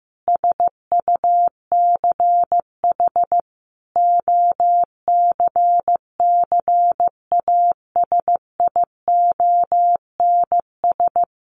Morse code, heard as a keyed tone, about 15 wpm, 700 Hz. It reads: SUCH OCCASIONS
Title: A Sound Of Thunder 15wpm